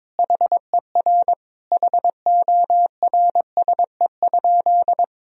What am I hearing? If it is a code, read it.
Morse code, heard as a keyed tone, about 22 wpm, 700 Hz. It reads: HER HORSE?
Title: Sentences from Top 1000 Words - Review Part 1 - 22wpm